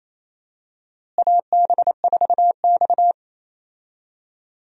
Morse code, heard as a keyed tone, about 28 wpm, 700 Hz. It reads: AB4X
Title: USA Call Signs - 4 Characters - Repeated 3x - 28wpm